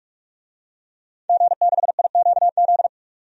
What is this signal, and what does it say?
Morse code, heard as a keyed tone, about 42 wpm, 700 Hz. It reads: G6IQZ
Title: DXCC-100 Call Signs - 5 Characters - Repeated 2x - 45wpm